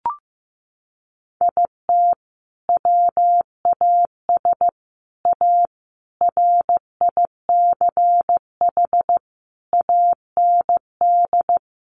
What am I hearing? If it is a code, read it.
Morse code, heard as a keyed tone, about 15 wpm, 700 Hz. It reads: IT WAS A RICH AND
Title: Sentences from Top 1100 Words 15wpm